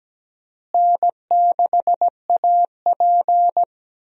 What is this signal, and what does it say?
Morse code, heard as a keyed tone, about 17 wpm, 700 Hz. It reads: N6AP
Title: USA Call Signs - 4 Characters - Repeated 2x - 17wpm